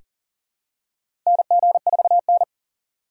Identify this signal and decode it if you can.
Morse code, heard as a keyed tone, about 40 wpm, 700 Hz. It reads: NG4N
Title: USA Call Signs - 3x Character Spacing - 40wpm